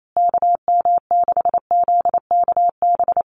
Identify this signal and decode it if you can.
Morse code, heard as a keyed tone, about 28 wpm, 700 Hz. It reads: KM6ZKB
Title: USA Call Signs Repeated 2x 28wpm